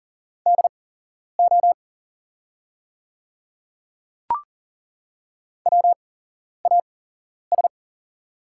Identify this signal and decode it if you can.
Morse code, heard as a keyed tone, about 40 wpm, spaced as for 11 wpm, 690 Hz. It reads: DO WAS
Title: 8x Character Spacing - Top 500 Words - 40wpm